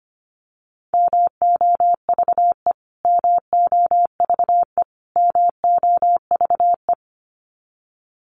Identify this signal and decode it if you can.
Morse code, heard as a keyed tone, about 25 wpm, 700 Hz. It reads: MOVE MOVE MOVE
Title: Top 500 Words Repeated 25wpm